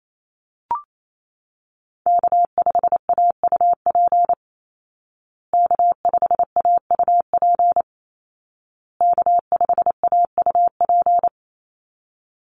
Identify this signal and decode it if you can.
Morse code, heard as a keyed tone, about 28 wpm, 700 Hz. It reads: K5AUP K5AUP K5AUP
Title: Call Signs Repeated 2x - 28wpm